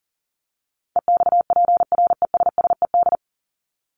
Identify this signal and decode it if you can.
Morse code, heard as a keyed tone, about 40 wpm, 700 Hz. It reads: EXPRESSED
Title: ED Suffix - Top 100 Words with Base Words 40wpm